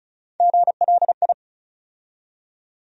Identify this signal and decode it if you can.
Morse code, heard as a keyed tone, about 36 wpm, 700 Hz. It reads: GLI
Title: USA Call Sign Postfixes - 3 Letters 35wpm